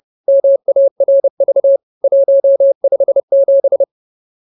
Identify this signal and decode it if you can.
Morse code, heard as a keyed tone, about 30 wpm, 550 Hz. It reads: MARV 157
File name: CWT-215-30